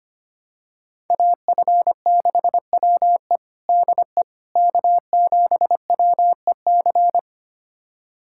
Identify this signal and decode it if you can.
Morse code, heard as a keyed tone, about 25 wpm, 700 Hz. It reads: AF6WE DE K7WEC
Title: USA Callsign DE Callsign - 5 Characters - 25wpm